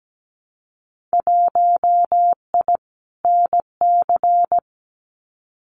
Morse code, heard as a keyed tone, about 17 wpm, 700 Hz. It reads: 1I NC
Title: Winter Field Day Exchanges 17wpm